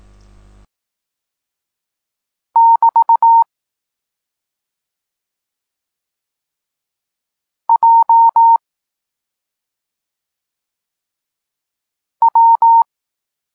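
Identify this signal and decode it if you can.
Morse code, heard as a keyed tone, about 18 wpm, 920 Hz. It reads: <BT> J W
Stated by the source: Morse code lesson for learning the letters K and X, from the SP9MOA Niepołomicki Klub Krótkofalowców course.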